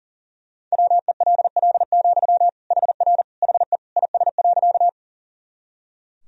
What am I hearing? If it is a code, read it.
Morse code, heard as a keyed tone, about 40 wpm, 700 Hz. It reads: WELL, HR HE IS.
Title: Sentences from Top 200 Words and Abbreviations 40wpm